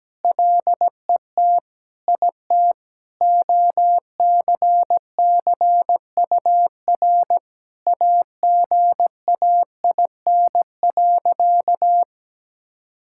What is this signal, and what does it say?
Morse code, heard as a keyed tone, about 17 wpm, 700 Hz. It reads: LET IT OCCUR AGAIN.
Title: Sentences from Top 1000 Words - Review Part 6 - 17wpm